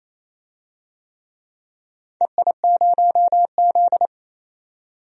Morse code, heard as a keyed tone, about 28 wpm, 700 Hz. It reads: EI0Z
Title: DXCC-100 Call Signs - 4 Characters - 28wpm